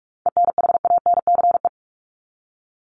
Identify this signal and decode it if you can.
Morse code, heard as a keyed tone, about 44 wpm, 700 Hz. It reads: ENHANCE
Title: Top 1251-1500 Words 45wpm